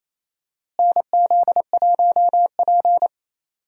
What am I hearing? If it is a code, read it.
Morse code, heard as a keyed tone, about 28 wpm, 700 Hz. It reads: NZ1P
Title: USA Call Signs - 4 Characters - Repeated 3x - 28wpm